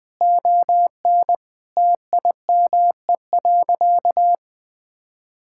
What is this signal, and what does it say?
Morse code, heard as a keyed tone, about 20 wpm, 700 Hz. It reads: ON TIME.
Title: Sentences with 4-Word Phrases - Spoken Beginning - 20wpm